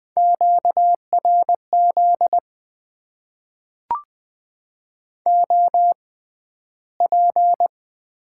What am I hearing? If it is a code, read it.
Morse code, heard as a keyed tone, about 20 wpm, spaced as for 12 wpm, 700 Hz. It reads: QRZ OP
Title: Top 90 QSO Elements - 6x Character Spacing - 20wpm